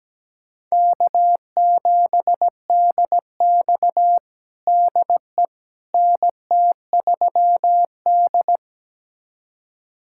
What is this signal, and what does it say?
Morse code, heard as a keyed tone, about 17 wpm, 700 Hz. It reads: K7DX DE NT3D
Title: USA Callsign DE Callsign - 4 Characters - 17wpm